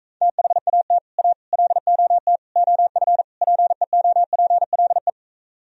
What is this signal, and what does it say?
Morse code, heard as a keyed tone, about 45 wpm, 700 Hz. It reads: THAT A LOT OF PEOPLE